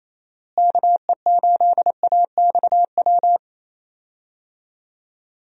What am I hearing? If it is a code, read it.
Morse code, heard as a keyed tone, about 28 wpm, 700 Hz. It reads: KE8AXW
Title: USA Call Signs - 6 Characters - Repeated 3x - 28wpm